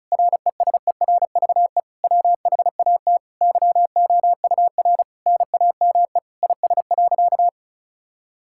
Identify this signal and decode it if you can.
Morse code, heard as a keyed tone, about 36 wpm, 700 Hz. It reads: RESERVE WHAT YOUR NAME IS.
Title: Sentences from Top 1000 Words - Review Part 6 - 35wpm